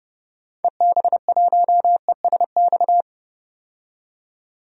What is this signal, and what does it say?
Morse code, heard as a keyed tone, about 30 wpm, 700 Hz. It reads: EB1ESX
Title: DXCC-100 Call Signs - 6 Characters - 30wpm